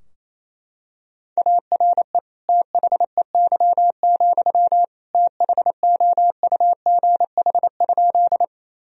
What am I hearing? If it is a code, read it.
Morse code, heard as a keyed tone, about 28 wpm, 700 Hz. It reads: ARE THEY, THOUGH?
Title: Sentences from Top 1000 Words - Review Part 2 - 28wpm